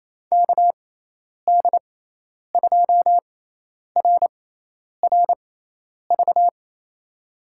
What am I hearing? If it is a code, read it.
Morse code, heard as a keyed tone, about 28 wpm, spaced as for 10 wpm, 700 Hz. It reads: KD2RRV
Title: USA Call Signs - 6 Characters - 6x Character Spacing - 28wpm